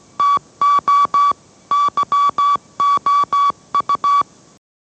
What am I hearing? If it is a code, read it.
Morse code, heard as a keyed tone, about 17 wpm, 1140 Hz. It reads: TO YOU
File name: mrx / mrx080 18wpm